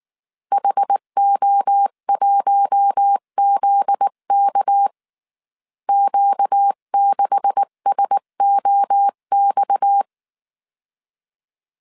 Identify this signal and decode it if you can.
Morse code, heard as a keyed tone, about 19 wpm, 800 Hz. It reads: HO1ZK Q6SOX
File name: CW / DET / 1 9 1500C S C 18wpm 2539